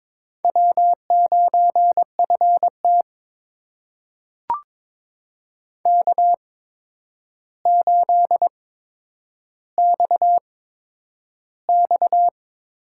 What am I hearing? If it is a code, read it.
Morse code, heard as a keyed tone, about 22 wpm, spaced as for 9 wpm, 700 Hz. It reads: W9FT K8XX
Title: USA Call Signs - 4 Characters - 8x Character Spacing - 22wpm